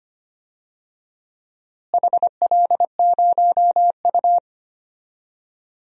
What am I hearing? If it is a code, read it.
Morse code, heard as a keyed tone, about 25 wpm, 700 Hz. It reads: HL0U
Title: DXCC-100 Call Signs - 4 Characters - Repeated 2x - 25wpm